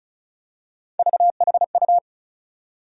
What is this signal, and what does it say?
Morse code, heard as a keyed tone, about 36 wpm, 700 Hz. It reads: VHU
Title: USA Call Sign Postfixes - 3 Letters 35wpm